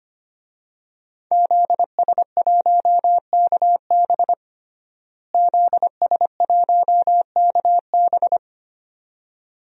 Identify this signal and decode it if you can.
Morse code, heard as a keyed tone, about 25 wpm, 700 Hz. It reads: ZS1KB ZS1KB
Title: DXCC-100 Call Signs - 5 Characters - Repeated 2x - 25wpm